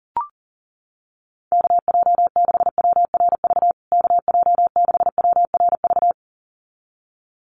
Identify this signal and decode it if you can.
Morse code, heard as a keyed tone, about 40 wpm, 700 Hz. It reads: KJ6WRV KJ6WRV
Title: USA Call Signs Repeated 2x 40wpm